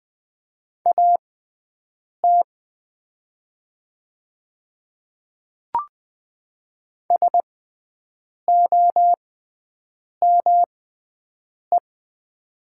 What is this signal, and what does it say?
Morse code, heard as a keyed tone, about 20 wpm, spaced as for 7 wpm, 700 Hz. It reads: AT SOME
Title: 6x Character Spacing - Top 100 Words - 20wpm